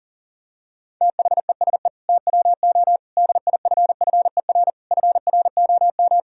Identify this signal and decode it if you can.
Morse code, heard as a keyed tone, about 40 wpm, 700 Hz. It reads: THESE TWO DIFFER FROM
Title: Sentences from Top 1000 Words - Review Part 1 - 40wpm